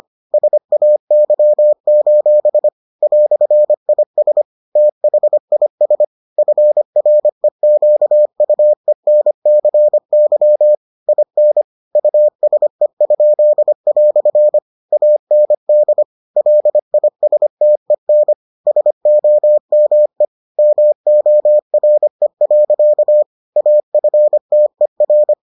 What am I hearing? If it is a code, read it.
Morse code, heard as a keyed tone, about 25 wpm, 600 Hz. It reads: SAY: "IS THIS FREQUENCY IN USE?" AND LISTEN SOME MORE. AFTER